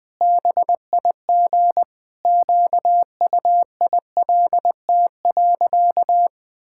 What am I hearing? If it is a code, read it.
Morse code, heard as a keyed tone, about 21 wpm, 700 Hz. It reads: BIG QUILT.
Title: Pangram Challenge v2 - 20wpm